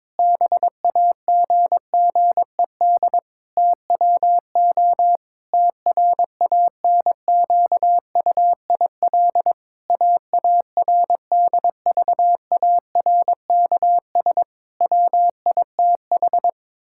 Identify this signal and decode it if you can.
Morse code, heard as a keyed tone, about 21 wpm, 700 Hz. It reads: BAGGED TWO TRANQUIL AARDVARKS WITH
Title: Pangram Challenge v2 - 22wpm